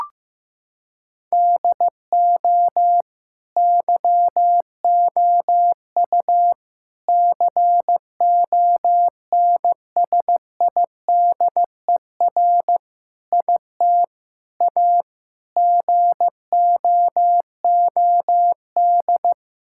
Morse code, heard as a keyed tone, about 15 wpm, 700 Hz. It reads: DO YOU CONSIDER IT A GOOD
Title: Sentences from Top 1600 Words 15wpm